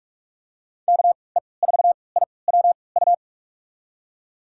Morse code, heard as a keyed tone, about 46 wpm, spaced as for 26 wpm, 700 Hz. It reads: KE4IWU
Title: USA Call Signs - 6 Characters - 3x Character Spacing - 45wpm